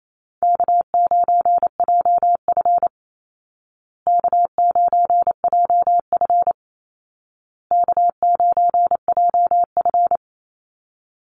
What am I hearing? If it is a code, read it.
Morse code, heard as a keyed tone, about 28 wpm, 700 Hz. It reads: K9JF K9JF K9JF
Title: Easy Call Signs Repeated 3x - 28wpm